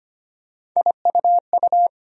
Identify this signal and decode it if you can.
Morse code, heard as a keyed tone, about 25 wpm, 700 Hz. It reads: IUU
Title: USA Call Sign Postfixes - 3 Letters 25wpm